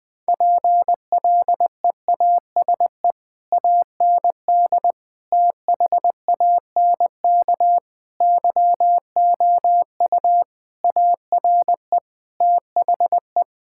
Morse code, heard as a keyed tone, about 20 wpm, 700 Hz. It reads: PLEASE AND THANK YOU ARE THE
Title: Difficult Sentences with Hints - 20wpm